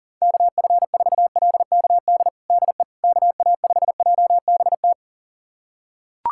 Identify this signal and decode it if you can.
Morse code, heard as a keyed tone, about 40 wpm, 700 Hz. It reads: KF4LKD DE KA5JBT
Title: USA Callsign DE Callsign - 6 Characters - 40wpm